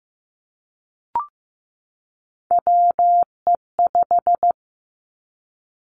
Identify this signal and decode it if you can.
Morse code, heard as a keyed tone, about 15 wpm, 700 Hz. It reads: WE5
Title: Three Challenging Letter-Number Combinations 15wpm